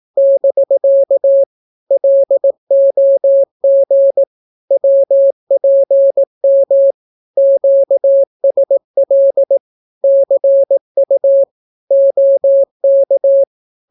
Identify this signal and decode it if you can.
Morse code, heard as a keyed tone, about 18 wpm, 550 Hz. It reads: <BK> LOG WPM QSL CU OK
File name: naukapodstaw / skroty / skroty